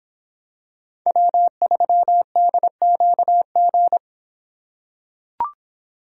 Morse code, heard as a keyed tone, about 26 wpm, 700 Hz. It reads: W3DQG